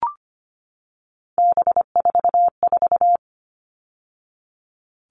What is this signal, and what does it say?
Morse code, heard as a keyed tone, about 25 wpm, 700 Hz. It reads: B44
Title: All Cut Numbers - 3-Digit 25wpm